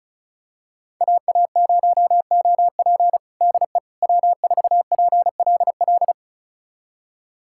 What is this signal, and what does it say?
Morse code, heard as a keyed tone, about 36 wpm, 700 Hz. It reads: AA0OP DE W4PLL
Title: USA Callsign DE Callsign - 5 Characters - 35wpm